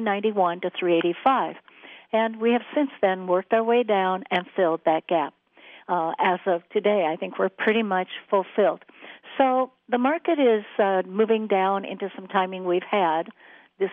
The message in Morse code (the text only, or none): none